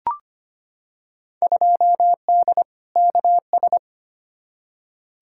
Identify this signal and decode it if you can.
Morse code, heard as a keyed tone, about 25 wpm, 700 Hz. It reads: 2D KS
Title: Field Day Exchanges Encore - 25wpm